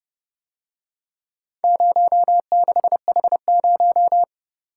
Morse code, heard as a keyed tone, about 30 wpm, 700 Hz. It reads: 06H0